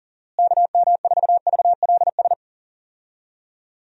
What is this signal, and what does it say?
Morse code, heard as a keyed tone, about 40 wpm, 700 Hz. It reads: KM4VLS
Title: USA Call Signs - 6 Characters - Repeated 3x - 40wpm